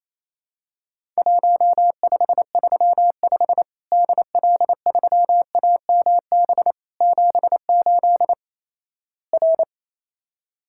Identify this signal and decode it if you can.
Morse code, heard as a keyed tone, about 28 wpm, 700 Hz. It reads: 1535 DL3AMB 78 R
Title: Worked All Europe DX Contest - QTC Exchange - 28wpm